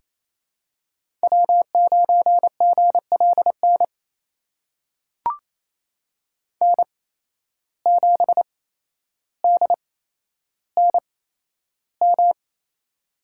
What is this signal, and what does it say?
Morse code, heard as a keyed tone, about 28 wpm, spaced as for 12 wpm, 700 Hz. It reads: W9GLN N7DNM
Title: USA Call Signs - 5 Characters - 8x Character Spacing - 28wpm